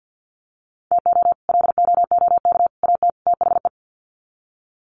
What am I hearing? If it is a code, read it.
Morse code, heard as a keyed tone, about 50 wpm, 700 Hz. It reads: TO LOOK AT THE
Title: Sets of 4 Words 50wpm